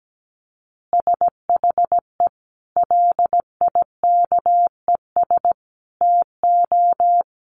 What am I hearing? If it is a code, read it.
Morse code, heard as a keyed tone, about 17 wpm, 700 Hz. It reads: SHE LIKES TO
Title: Additional Easy Phrases 17wpm